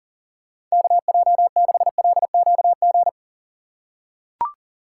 Morse code, heard as a keyed tone, about 40 wpm, 700 Hz. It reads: KJ6LQG
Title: USA Call Signs - 4x Character Spacing - 40wpm